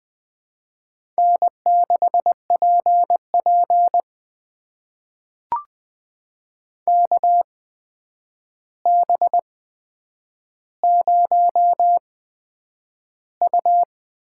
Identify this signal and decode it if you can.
Morse code, heard as a keyed tone, about 20 wpm, spaced as for 9 wpm, 700 Hz. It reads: N6PP KB0U
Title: USA Call Signs - 8x Character Spacing - 20wpm